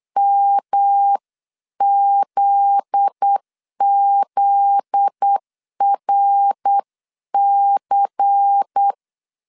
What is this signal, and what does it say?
Morse code, heard as a keyed tone, about 8 wpm, 800 Hz. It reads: MZZRC
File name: LEZIONI / 7 GF9Z / 7 GF9Z 8wpm 3213